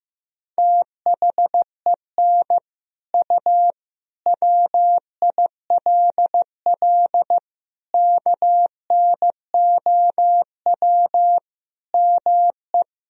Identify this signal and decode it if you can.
Morse code, heard as a keyed tone, about 15 wpm, 700 Hz. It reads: THEN U WILL KNOW ME
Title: Sentences from Top 400 Words and Abbreviations 15wpm